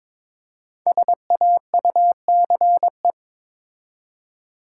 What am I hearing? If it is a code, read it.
Morse code, heard as a keyed tone, about 22 wpm, 700 Hz. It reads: SAUCE
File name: 5 Letter Words - First 3 Letters Given - 22wpm